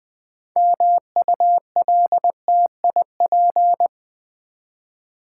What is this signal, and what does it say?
Morse code, heard as a keyed tone, about 20 wpm, 700 Hz. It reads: MULTIP
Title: Build A Word - 3 Letters At A Time - 8-Letter Words - 20wpm